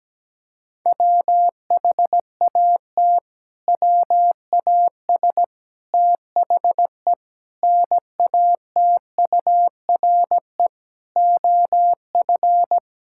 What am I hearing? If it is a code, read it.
Morse code, heard as a keyed tone, about 17 wpm, 700 Hz. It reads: WHAT WAS THE NATURE OF
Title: Sentences from Top 1600 Words 17wpm